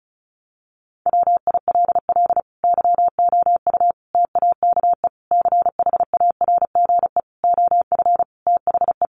Morse code, heard as a keyed tone, about 34 wpm, 700 Hz. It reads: WILL YOU TAKE CHARGE OF THE
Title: Sentences from Top 900 Words 35wpm